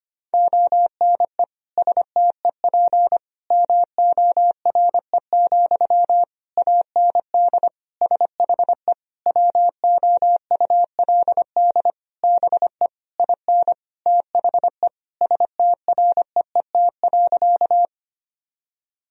Morse code, heard as a keyed tone, about 25 wpm, 700 Hz. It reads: ONE STEP MORE, AND SHE WOULD BE IN THE STREET.
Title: Sentences from Top 1000 Words - Review Part 3 - 25wpm